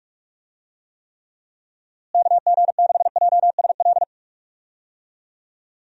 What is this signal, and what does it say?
Morse code, heard as a keyed tone, about 46 wpm, 700 Hz. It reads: KG6JSL
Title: USA Call Signs - 6 Characters - Repeated 3x - 45wpm